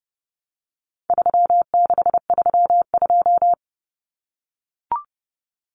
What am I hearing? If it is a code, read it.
Morse code, heard as a keyed tone, about 30 wpm, 700 Hz. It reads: 3632
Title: Common Cut Numbers - 4-Digit 30wpm